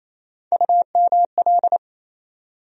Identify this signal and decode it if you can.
Morse code, heard as a keyed tone, about 28 wpm, 700 Hz. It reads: UML
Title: USA Call Sign Postfixes - 3 Letters 28wpm